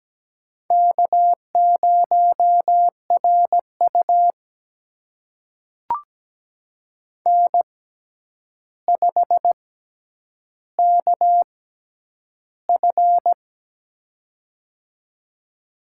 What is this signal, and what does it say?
Morse code, heard as a keyed tone, about 17 wpm, spaced as for 9 wpm, 700 Hz. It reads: K0RU N5KF
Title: USA Call Signs - 6x Character Spacing - 17wpm